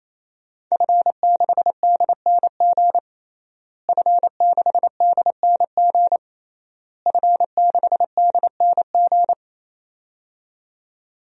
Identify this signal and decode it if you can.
Morse code, heard as a keyed tone, about 28 wpm, 700 Hz. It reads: F6DNG F6DNG F6DNG
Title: DXCC-100 Call Signs - 5 Characters - Repeated 3x - 28wpm